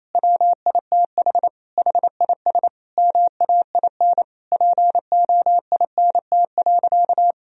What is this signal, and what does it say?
Morse code, heard as a keyed tone, about 28 wpm, 700 Hz. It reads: WITH HIS MAIN POINT.
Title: Sentences from Top 1000 Words - Review Part 2 - 28wpm